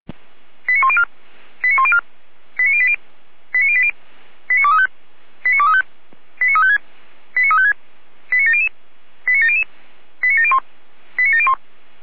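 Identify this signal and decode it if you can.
Five-tone selective call (ZVEI), heard as quick runs of five tones, tones of about 70 ms. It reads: 89104 89104 89080 89080 89235 89235 89356 89356 89700 89700 89701 89701
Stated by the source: ZVEI five-tone selective call received on an SDR.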